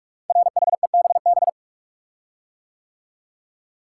Morse code, heard as a keyed tone, about 44 wpm, 700 Hz. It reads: RHEBB
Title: Code Groups - 5 Letters - 45wpm